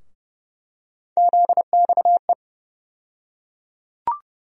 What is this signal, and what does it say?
Morse code, heard as a keyed tone, about 30 wpm, 700 Hz. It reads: ZXE
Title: USA Call Sign Postfixes - 3 Letters 30wpm